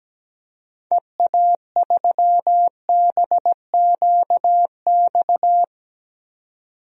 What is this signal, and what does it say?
Morse code, heard as a keyed tone, about 17 wpm, 700 Hz. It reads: EA3BQX
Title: DXCC-100 Call Signs - 6 Characters - 17wpm